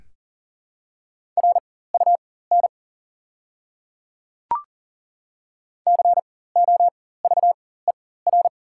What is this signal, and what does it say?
Morse code, heard as a keyed tone, about 40 wpm, spaced as for 19 wpm, 700 Hz. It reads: RUN COVER
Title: 4x Character Spacing - Top 300 Words - 40wpm